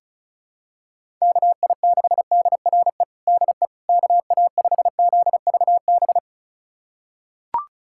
Morse code, heard as a keyed tone, about 34 wpm, 700 Hz. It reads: KI6DRE DE KA5ZVB
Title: USA Callsign DE Callsign 35wpm